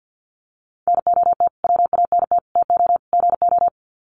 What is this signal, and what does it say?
Morse code, heard as a keyed tone, about 49 wpm, 700 Hz. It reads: NOT WANT TO GO
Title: Sets of 4 Words 50wpm